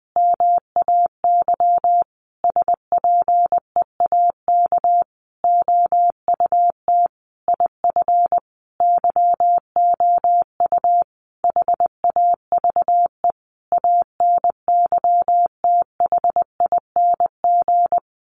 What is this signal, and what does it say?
Morse code, heard as a keyed tone, about 20 wpm, 700 Hz. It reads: MAY SPEAK OUT IF YOU HAVE ANYTHING
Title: Sentences from Top 1300 Words 20wpm